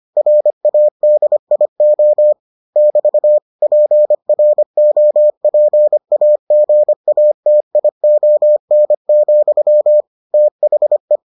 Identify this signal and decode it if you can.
Morse code, heard as a keyed tone, about 25 wpm, 600 Hz. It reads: RADIO <BT> PROPAGATION, THE